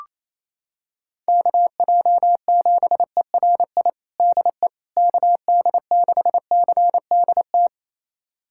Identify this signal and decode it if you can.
Morse code, heard as a keyed tone, about 28 wpm, 700 Hz. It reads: KJ7ERI DE KD6CDT
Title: USA Callsign DE Callsign 28wpm